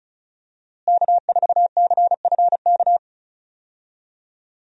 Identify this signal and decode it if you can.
Morse code, heard as a keyed tone, about 36 wpm, 700 Hz. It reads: K4CFK
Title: USA Call Signs - 5 Characters - Repeated 3x - 35wpm